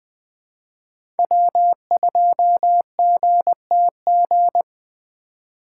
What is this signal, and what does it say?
Morse code, heard as a keyed tone, about 20 wpm, 700 Hz. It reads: W2GTG
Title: Speed Racing - USA 1x3 Call Signs 17wpm